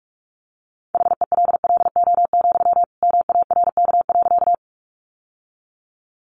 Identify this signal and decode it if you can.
Morse code, heard as a keyed tone, about 44 wpm, 700 Hz. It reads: HELLO, MARK.
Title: Sentences from Top 1100 Words 45wpm